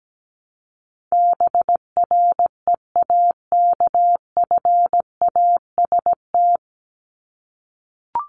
Morse code, heard as a keyed tone, about 17 wpm, 700 Hz. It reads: BREAKFAST
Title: Top 1251-1500 Words 17wpm